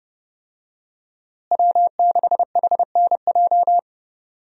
Morse code, heard as a keyed tone, about 30 wpm, 700 Hz. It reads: W6HNJ
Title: USA Call Signs - 5 Characters - Repeated 3x - 30wpm